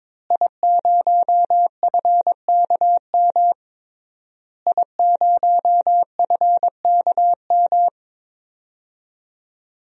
Morse code, heard as a keyed tone, about 22 wpm, 700 Hz. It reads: I0FKM I0FKM
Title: DXCC-100 Call Signs - 5 Characters - Repeated 3x - 22wpm